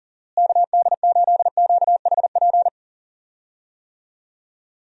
Morse code, heard as a keyed tone, about 40 wpm, 700 Hz. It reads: KD8QHP
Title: USA Call Signs - 6 Characters - Repeated 2x - 40wpm